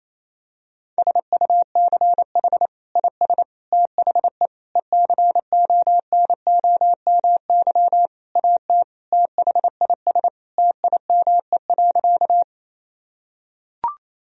Morse code, heard as a keyed tone, about 28 wpm, 700 Hz. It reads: SUCH IS THE ECONOMY AT THIS TIME.
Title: Sentences from Top 1000 Words - Review Part 3 - 28wpm